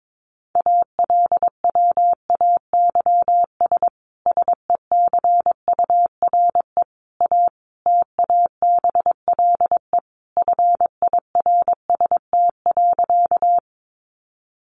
Morse code, heard as a keyed tone, about 22 wpm, 700 Hz. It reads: ALWAYS SECURE A TABLE FIRST.
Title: Sentences from Top 1200 Words 22wpm